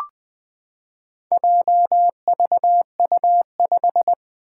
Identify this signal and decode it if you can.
Morse code, heard as a keyed tone, about 20 wpm, 700 Hz. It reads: JVU5
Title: Four Challenging Letter-Number Combinations 20wpm